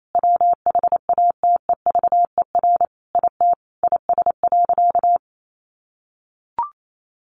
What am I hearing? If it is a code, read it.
Morse code, heard as a keyed tone, about 28 wpm, 700 Hz. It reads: WHATEVER IT IS.
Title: Sentences from Top 1100 Words 28wpm